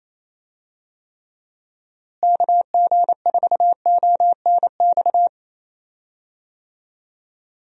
Morse code, heard as a keyed tone, about 28 wpm, 700 Hz. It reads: KG4ONX
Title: USA Call Signs - 6 Characters - Repeated 2x - 28wpm